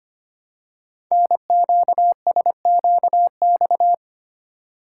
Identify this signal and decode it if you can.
Morse code, heard as a keyed tone, about 25 wpm, 700 Hz. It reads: NQSQX
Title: Code Groups - 5 Letters - 25wpm